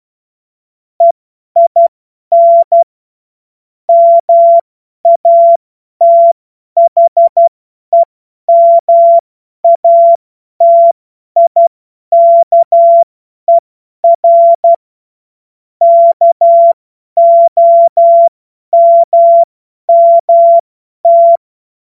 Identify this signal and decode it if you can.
Morse code, heard as a keyed tone, about 12 wpm, spaced as for 10 wpm, 680 Hz. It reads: EIN MATHEMATIKER KOMMT
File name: Morsekurs / Witze / Witz 07. Juni 2022 50 F70